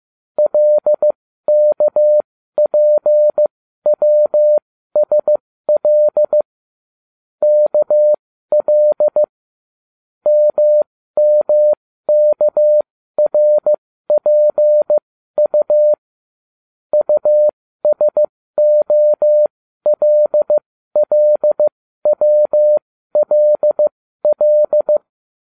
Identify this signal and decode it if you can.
Morse code, heard as a keyed tone, about 15 wpm, spaced as for 12 wpm, 600 Hz. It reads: LKPWSL KL MMKRPU USOLLWLL